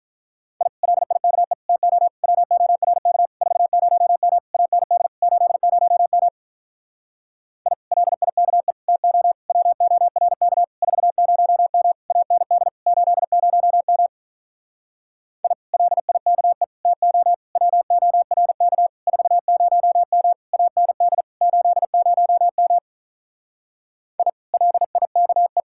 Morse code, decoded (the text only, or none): I LIKE TO WORK 40M AND 80M I LIKE TO WORK 40M AND 80M I LIKE TO WORK 40M AND 80M I LIKE